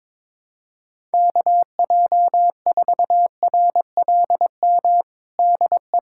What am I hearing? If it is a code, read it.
Morse code, heard as a keyed tone, about 22 wpm, 700 Hz. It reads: KJ4RLM DE
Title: USA Callsign DE Callsign - 6 Characters - 22wpm